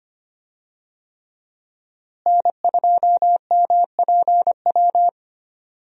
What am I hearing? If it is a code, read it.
Morse code, heard as a keyed tone, about 25 wpm, 700 Hz. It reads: N2MPW
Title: USA Call Signs - 5 Characters - Repeated 3x - 25wpm